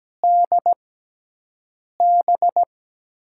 Morse code, heard as a keyed tone, about 17 wpm, spaced as for 6 wpm, 700 Hz. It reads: DB
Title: USA Call Signs - 4 Characters - 6x Character Spacing - 17wpm